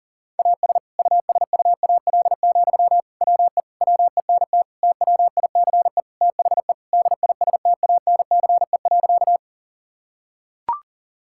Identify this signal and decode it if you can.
Morse code, heard as a keyed tone, about 40 wpm, 700 Hz. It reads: AS USUAL, WE WENT TWICE THE DISTANCE.
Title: Sentences from Top 1600 Words 40wpm